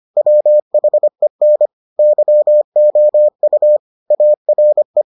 Morse code, heard as a keyed tone, about 25 wpm, 600 Hz. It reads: WHEN YOU ARE